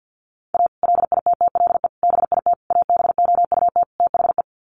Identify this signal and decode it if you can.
Morse code, heard as a keyed tone, about 50 wpm, 700 Hz. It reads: A LITTLE BIT ABOUT THE
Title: Sets of 5 Words Encore2 50wpm